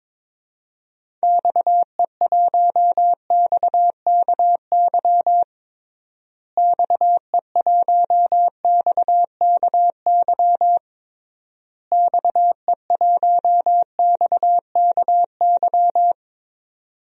Morse code, heard as a keyed tone, about 22 wpm, 700 Hz. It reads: XE1XKY XE1XKY XE1XKY
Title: DXCC-100 Call Signs - 6 Characters - Repeated 3x - 22wpm